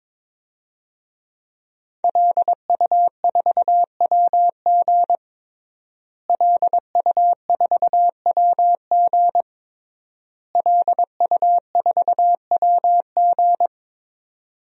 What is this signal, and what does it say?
Morse code, heard as a keyed tone, about 22 wpm, 700 Hz. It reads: LU4WG LU4WG LU4WG
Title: DXCC-100 Call Signs - 5 Characters - Repeated 3x - 22wpm